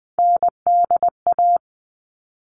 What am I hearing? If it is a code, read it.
Morse code, heard as a keyed tone, about 20 wpm, 700 Hz. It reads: NDA
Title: All Cut Numbers - 3-Digit 20wpm